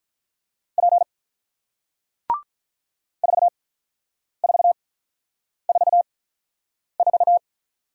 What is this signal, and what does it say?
Morse code, heard as a keyed tone, about 48 wpm, spaced as for 9 wpm, 700 Hz. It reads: P 4444